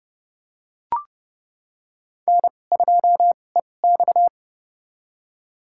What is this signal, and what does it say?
Morse code, heard as a keyed tone, about 30 wpm, spaced as for 22 wpm, 700 Hz. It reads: N2EX
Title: USA Call Signs - 4 Characters - 2x Character Spacing - 30wpm